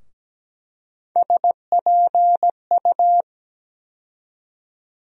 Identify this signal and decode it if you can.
Morse code, heard as a keyed tone, about 17 wpm, 700 Hz. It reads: SPU
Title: USA Call Sign Postfixes - 3 Letters 17wpm